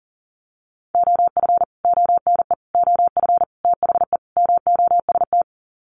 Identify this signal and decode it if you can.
Morse code, heard as a keyed tone, about 40 wpm, 700 Hz. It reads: OF ONE OF THE MOST
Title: Sets of 5 Words Encore2 40wpm